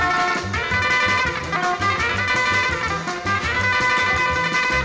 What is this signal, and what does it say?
Emergency Alert System (SAME) tones, heard as harsh buzzing bursts. It is heard from 0.6 to 4.2 s.